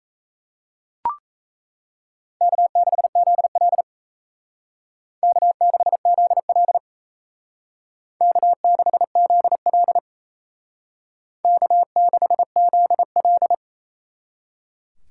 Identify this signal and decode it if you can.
Morse code, heard as a keyed tone, about 36 wpm, 700 Hz. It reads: K6ZL K6ZL K6ZL K6ZL